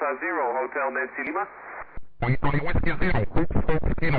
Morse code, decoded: none